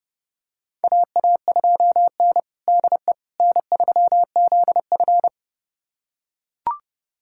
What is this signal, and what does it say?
Morse code, heard as a keyed tone, about 32 wpm, 700 Hz. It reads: AA2N DE N3ZF
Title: USA Callsign DE Callsign - 4 Characters - 30wpm